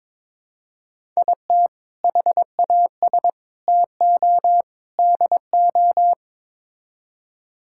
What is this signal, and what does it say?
Morse code, heard as a keyed tone, about 22 wpm, 700 Hz. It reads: IT HAS TO DO
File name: practice / 22wpm / Build A Phrase - 5-Word Phrases - 22wpm